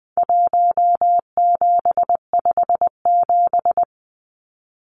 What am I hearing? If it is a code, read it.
Morse code, heard as a keyed tone, about 20 wpm, 700 Hz. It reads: 1757
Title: Numbers - Contest Style - 4-Digit 20wpm